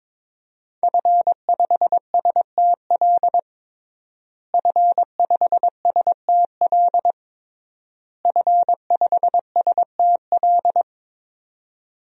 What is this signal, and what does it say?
Morse code, heard as a keyed tone, about 22 wpm, 700 Hz. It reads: F5STL F5STL F5STL
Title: DXCC-100 Call Signs - 5 Characters - Repeated 3x - 22wpm